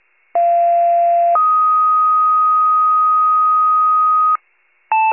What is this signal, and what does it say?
Two-tone page, heard as a one-second tone, then a three-second tone, about 690 then 1250 Hz. It is heard from 0.3 to 4.4 s.